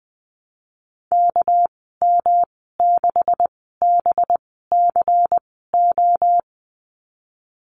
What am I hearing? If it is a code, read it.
Morse code, heard as a keyed tone, about 20 wpm, spaced as for 14 wpm, 700 Hz. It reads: KM6BCO
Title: 213 Lesson 34g - Number 8 - Call Signs 20wpm